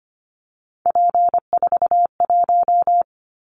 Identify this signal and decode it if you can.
Morse code, heard as a keyed tone, about 25 wpm, 700 Hz. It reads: P41
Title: Callsign Prefixes for Top 100 DXCC - 25wpm